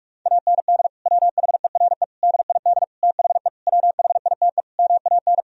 Morse code, heard as a keyed tone, about 47 wpm, 700 Hz. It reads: AND WHERE DID THE WHITE MAN
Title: Sentences from Top 1000 Words - Review Part 1 - 45wpm